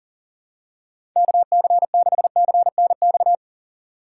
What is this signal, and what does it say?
Morse code, heard as a keyed tone, about 40 wpm, 700 Hz. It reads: KC6CNX
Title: USA Call Signs - 6 Characters - 8x Character Spacing - 40wpm